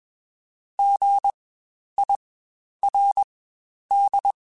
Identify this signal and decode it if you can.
Morse code, heard as a keyed tone, about 21 wpm, spaced as for 10 wpm, 800 Hz. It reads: GIRD
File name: Spaced Alphas / Spaced Alphas mp3s / Gs sp @20